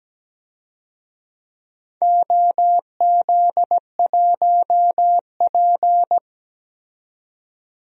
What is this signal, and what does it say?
Morse code, heard as a keyed tone, about 17 wpm, 700 Hz. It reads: OZ1P
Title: DXCC-100 Call Signs - 4 Characters - 17wpm